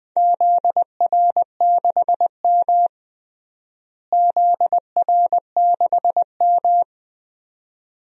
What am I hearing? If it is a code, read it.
Morse code, heard as a keyed tone, about 20 wpm, 700 Hz. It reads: ZR6M ZR6M
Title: DXCC-100 Call Signs - 4 Characters - Repeated 2x - 20wpm